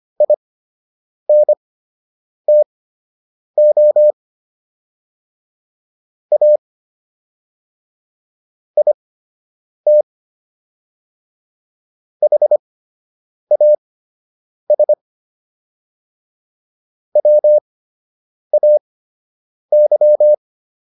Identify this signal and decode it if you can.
Morse code, heard as a keyed tone, about 25 wpm, spaced as for 8 wpm, 600 Hz. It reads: INTO A IT HAS WAY